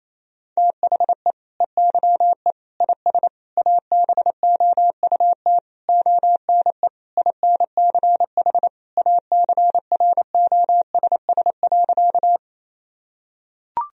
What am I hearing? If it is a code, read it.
Morse code, heard as a keyed tone, about 28 wpm, 700 Hz. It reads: THE EYE IS ABOUT ONE INCH ACROSS.
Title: Sentences from Top 1000 Words - Review Part 3 - 28wpm